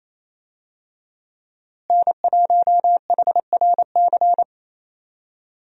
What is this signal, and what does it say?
Morse code, heard as a keyed tone, about 28 wpm, 700 Hz. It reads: N1HRC
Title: USA Call Signs - 5 Characters - Repeated 3x - 28wpm